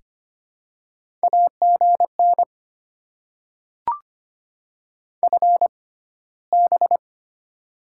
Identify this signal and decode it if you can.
Morse code, heard as a keyed tone, about 25 wpm, spaced as for 15 wpm, 700 Hz. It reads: AGN FB
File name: Top 20 QSO Elements - 6x Character Spacing - 25wpm